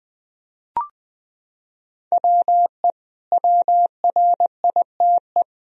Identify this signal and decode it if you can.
Morse code, heard as a keyed tone, about 21 wpm, 700 Hz. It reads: WE WRITE
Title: Sentences from Top 1000 Words - Review Part 3 - 20wpm